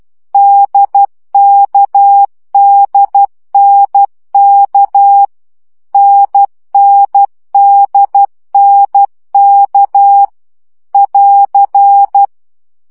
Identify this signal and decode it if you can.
Morse code, heard as a keyed tone, about 12 wpm, 800 Hz. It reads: DKDNK NNDNK <AR>
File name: CW gyakorlas n d k 10 baud